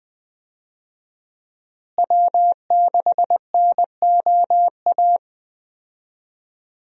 Morse code, heard as a keyed tone, about 20 wpm, 700 Hz. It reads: W6NOA
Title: USA Call Signs - 5 Characters - Repeated 2x - 20wpm